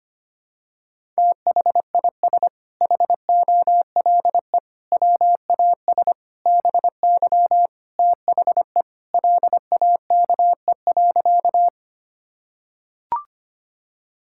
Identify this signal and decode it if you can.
Morse code, heard as a keyed tone, about 25 wpm, 700 Hz. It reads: THIS HOLE WAS BY THE LAKE.
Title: Sentences from Top 1600 Words 25wpm